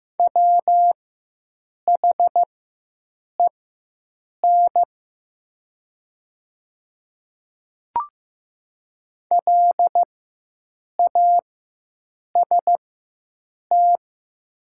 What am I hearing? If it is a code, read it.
Morse code, heard as a keyed tone, about 15 wpm, spaced as for 7 wpm, 700 Hz. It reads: WHEN LAST
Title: 4x Character Spacing - Top 500 Words - 15wpm